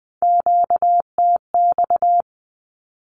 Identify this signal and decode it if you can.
Morse code, heard as a keyed tone, about 20 wpm, 700 Hz. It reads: QTX
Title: Q-Codes 20wpm